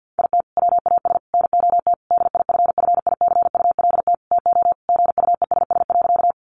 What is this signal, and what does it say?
Morse code, heard as a keyed tone, about 50 wpm, 700 Hz. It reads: IT WAS NOT DIFFICULT TO GUESS.
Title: Sentences from Top 1500 Words 50wpm